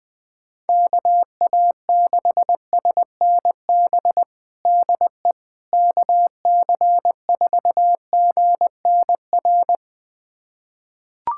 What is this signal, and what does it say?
Morse code, heard as a keyed tone, about 20 wpm, 700 Hz. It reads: KA6SNB DE KC4GNR
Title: USA Callsign DE Callsign - 6 Characters - 20wpm